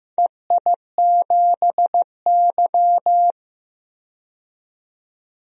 Morse code, heard as a keyed tone, about 15 wpm, 700 Hz. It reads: EI7Y
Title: DXCC-100 Call Signs - 4 Characters - 15wpm